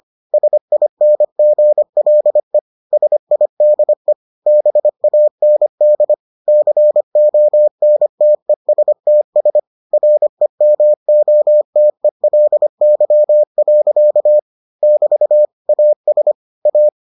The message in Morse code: SINGLE SIDE BAND CONTESTS REMOTELY. <BT> AS A

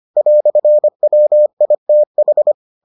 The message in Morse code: "WITH

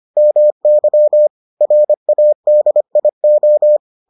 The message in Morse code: MY RADIO

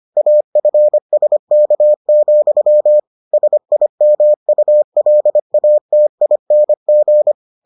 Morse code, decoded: AFSK, SIMULATING

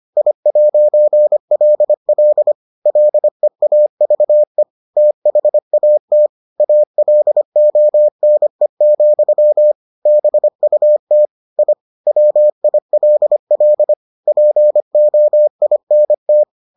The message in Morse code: I'LL LEAVE THAT ALONE, BUT I WILL POINT

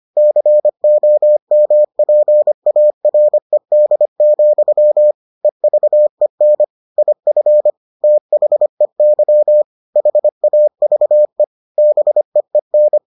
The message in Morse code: COMPARED, EVEN IF THEY HAVE BEEN

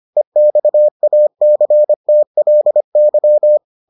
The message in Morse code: EXACTLY

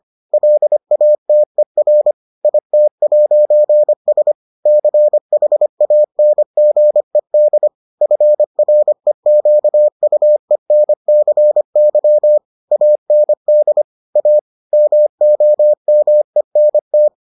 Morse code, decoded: LATER IT'S CHANGED FREQUENCY AND A MOMENT

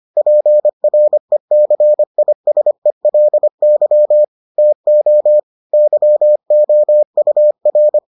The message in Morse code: PRECISELY TO YOUR

